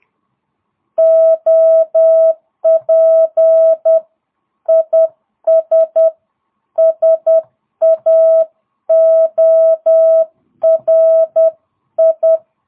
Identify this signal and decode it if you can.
Morse code, heard as a keyed tone, about 10 wpm, 650 Hz. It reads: OP IS SAORI